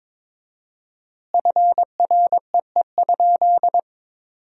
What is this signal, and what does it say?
Morse code, heard as a keyed tone, about 22 wpm, 700 Hz. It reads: FREE?
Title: 4x Word-Spacing - Sentences from Top 500 Words - 22wpm